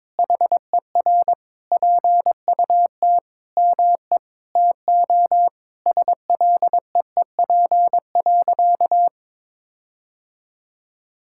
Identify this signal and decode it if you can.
Morse code, heard as a keyed tone, about 22 wpm, 700 Hz. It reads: HER PUT ME TO SLEEP.
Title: Sentences from Top 1600 Words 22wpm